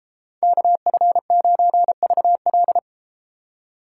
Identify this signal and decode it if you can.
Morse code, heard as a keyed tone, about 32 wpm, 700 Hz. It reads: KF9VL